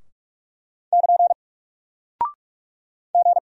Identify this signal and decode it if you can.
Morse code, heard as a keyed tone, about 43 wpm, 700 Hz. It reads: <KN> G
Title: Single Character (Letter:Number:Punctuation:Prosign) Alt 45wpm